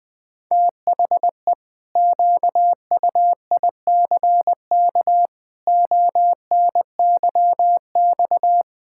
Morse code, heard as a keyed tone, about 20 wpm, 700 Hz. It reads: THE QUICK ONYX
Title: Pangram Challenge v2 - 20wpm